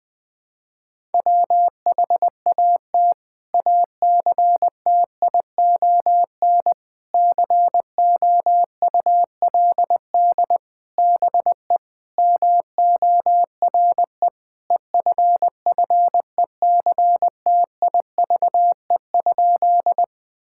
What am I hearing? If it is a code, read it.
Morse code, heard as a keyed tone, about 21 wpm, 700 Hz. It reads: WHAT ACTION COULD BE MORE EFFECTIVE?
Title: Sentences from Top 1000 Words - Review Part 6 - 20wpm